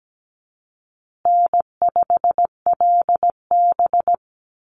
Morse code, heard as a keyed tone, about 17 wpm, 700 Hz. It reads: N5LB
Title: Call Signs Repeated 3x - 17wpm